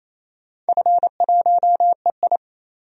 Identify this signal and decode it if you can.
Morse code, heard as a keyed tone, about 28 wpm, 700 Hz. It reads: F1EI
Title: DXCC-100 Call Signs - 4 Characters - 28wpm